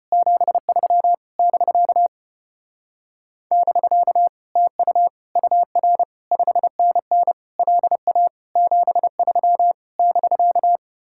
Morse code, heard as a keyed tone, about 33 wpm, 700 Hz. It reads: 73 <BK> <BK> TU UR 5NN LA 73 <BK>